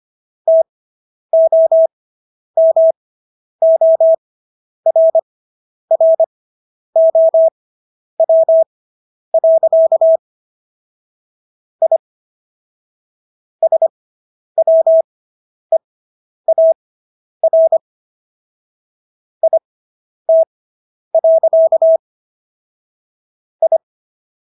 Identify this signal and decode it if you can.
Morse code, heard as a keyed tone, about 25 wpm, spaced as for 10 wpm, 650 Hz. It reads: TOMORROW. I SWEAR IT. I